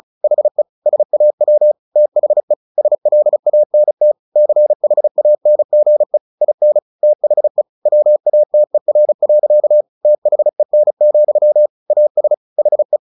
Morse code, heard as a keyed tone, about 34 wpm, 600 Hz. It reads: HE SAW THE SLANT CHANGE IN THE WATER. THEN, AS HE
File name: the old man and the sea / the old man and the sea 1 35wpm